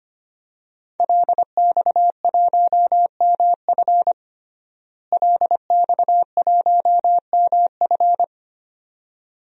Morse code, heard as a keyed tone, about 25 wpm, 700 Hz. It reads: LX1MF LX1MF
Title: DXCC-100 Call Signs - 5 Characters - Repeated 2x - 25wpm